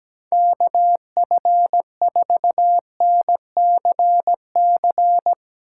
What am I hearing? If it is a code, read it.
Morse code, heard as a keyed tone, about 17 wpm, 700 Hz. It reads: KF4NCC
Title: Speed Racing - USA 2x3 Call Signs 17wpm